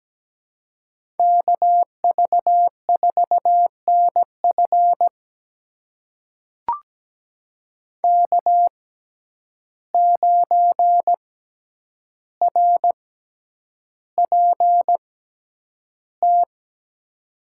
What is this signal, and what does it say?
Morse code, heard as a keyed tone, about 17 wpm, spaced as for 9 wpm, 700 Hz. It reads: KV4NF K9RPT
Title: USA Call Signs - 5 Characters - 6x Character Spacing - 17wpm